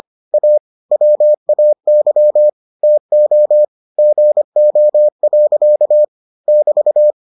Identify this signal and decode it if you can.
Morse code, heard as a keyed tone, about 25 wpm, 600 Hz. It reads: A WAY TO GO. <BT>